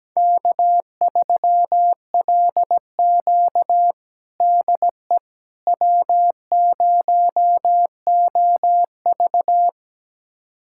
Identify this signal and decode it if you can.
Morse code, heard as a keyed tone, about 17 wpm, 700 Hz. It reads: K3LQ DE W0OV
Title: USA Callsign DE Callsign - 4 Characters - 17wpm